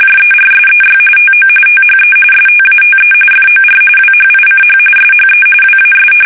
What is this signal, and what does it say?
RTTY, about 45.45 baud. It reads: JERSEY TO THE STATE FOR WHICH SHE WAS
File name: fsk850 / battleshipnj